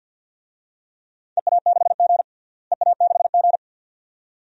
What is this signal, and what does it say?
Morse code, heard as a keyed tone, about 47 wpm, 700 Hz. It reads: EA6G EA6G
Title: DXCC-100 Call Signs - 4 Characters - Repeated 2x - 50wpm